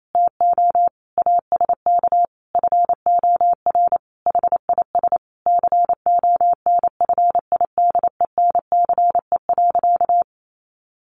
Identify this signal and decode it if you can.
Morse code, heard as a keyed tone, about 28 wpm, 700 Hz. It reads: TO ASK FOR HIS CONFIDENCE.
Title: Sentences from Top 1500 Words 28wpm